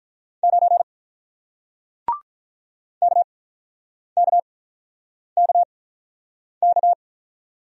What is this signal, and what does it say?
Morse code, heard as a keyed tone, about 48 wpm, spaced as for 9 wpm, 700 Hz. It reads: 9 KKKK